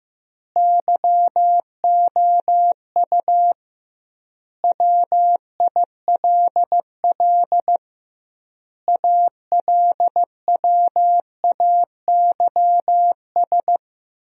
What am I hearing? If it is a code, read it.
Morse code, heard as a keyed tone, about 16 wpm, 700 Hz. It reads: YOU WILL ALWAYS
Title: 2x Word-Spacing - Sentences from Top 400 Words - 15wpm